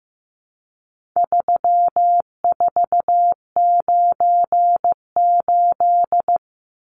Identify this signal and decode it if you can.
Morse code, heard as a keyed tone, about 15 wpm, 700 Hz. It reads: 3498
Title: Numbers - 4-Digit 15wpm